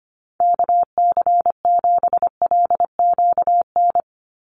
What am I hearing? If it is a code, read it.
Morse code, heard as a keyed tone, about 25 wpm, 700 Hz. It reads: KC7LQN
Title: USA Call Signs 25wpm